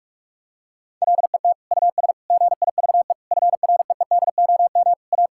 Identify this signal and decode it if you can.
Morse code, heard as a keyed tone, about 46 wpm, 700 Hz. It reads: LET US GIVE FREEDOM A
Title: Sentences from Top 1000 Words - Review Part 5 - 45wpm